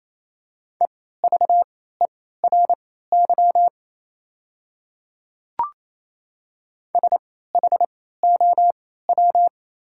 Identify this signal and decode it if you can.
Morse code, heard as a keyed tone, about 27 wpm, spaced as for 16 wpm, 700 Hz. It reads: EVERY SHOW
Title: 3x Character Spacing - Top 200 Words - 28wpm